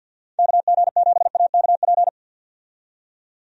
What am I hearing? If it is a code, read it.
Morse code, heard as a keyed tone, about 51 wpm, 700 Hz. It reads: KG7AKP
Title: USA Call Signs - 6 Characters - 50wpm